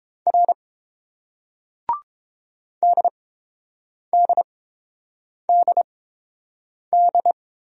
Morse code, heard as a keyed tone, about 30 wpm, spaced as for 7 wpm, 700 Hz. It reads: R DDDD